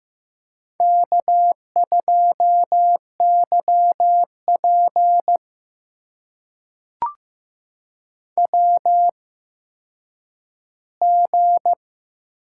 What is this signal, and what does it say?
Morse code, heard as a keyed tone, about 15 wpm, spaced as for 9 wpm, 700 Hz. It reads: K2YP WG
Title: USA Call Signs - 4 Characters - 8x Character Spacing - 15wpm